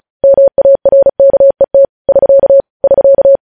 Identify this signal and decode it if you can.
Morse code, heard as a keyed tone, about 36 wpm, 550 Hz. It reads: MARKET <SK> <SK>
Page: news headlines in morse code at 35 wpm